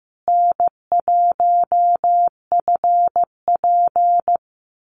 Morse code, heard as a keyed tone, about 15 wpm, 700 Hz. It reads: N1FP
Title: USA Call Signs Repeated 3x 15wpm